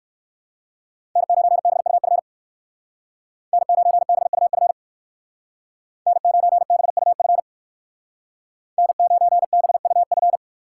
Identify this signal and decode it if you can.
Morse code, heard as a keyed tone, about 58 wpm, 700 Hz. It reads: N9BUF N9BUF N9BUF N9BUF